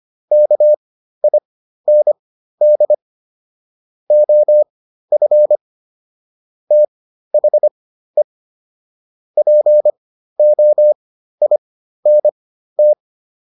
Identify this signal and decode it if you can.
Morse code, heard as a keyed tone, about 25 wpm, spaced as for 13 wpm, 600 Hz. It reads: KIND OF THE POINT
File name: PR402 13